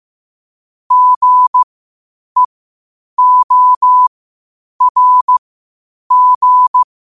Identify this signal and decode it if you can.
Morse code, heard as a keyed tone, about 15 wpm, spaced as for 8 wpm, 1000 Hz. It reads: GEORG
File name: qotdfiles / 8wpm